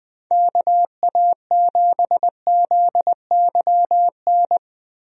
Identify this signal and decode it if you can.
Morse code, heard as a keyed tone, about 20 wpm, 700 Hz. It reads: KA7ZYN
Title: USA Call Signs - 6 Characters - Repeated 2x - 20wpm